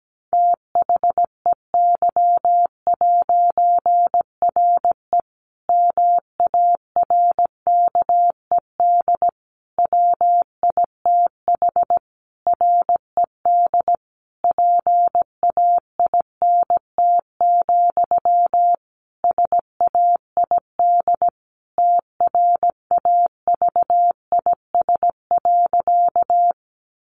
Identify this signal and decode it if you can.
Morse code, heard as a keyed tone, about 17 wpm, 700 Hz. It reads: THEY'RE MARKED WITH RED PAINT, SAID TRAVIS.
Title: A Sound Of Thunder 17wpm